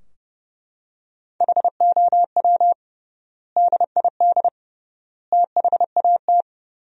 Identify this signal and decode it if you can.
Morse code, heard as a keyed tone, about 30 wpm, 700 Hz. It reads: HOW DID THAT
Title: 3x Word-Spacing - Sentences from Top 200 Words - 30wpm